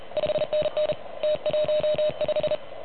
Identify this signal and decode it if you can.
Morse code, heard as a keyed tone, about 31 wpm, 610 Hz. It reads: 5NN T15
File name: truSDX / MWC29.8